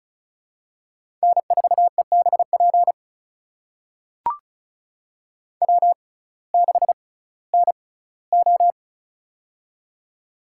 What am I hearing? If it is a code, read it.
Morse code, heard as a keyed tone, about 35 wpm, spaced as for 19 wpm, 700 Hz. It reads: N4EBP W6NO
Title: USA Call Signs - 6x Character Spacing - 35wpm